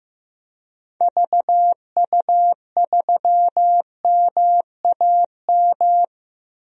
Morse code, heard as a keyed tone, about 15 wpm, 700 Hz. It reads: VU3MAM
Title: DXCC-100 Call Signs - 6 Characters - Repeated 2x - 15wpm